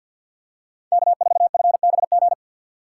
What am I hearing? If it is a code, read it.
Morse code, heard as a keyed tone, about 53 wpm, 700 Hz. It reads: K4FBG
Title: USA Call Signs - 5 Characters - 4x Character Spacing - 50wpm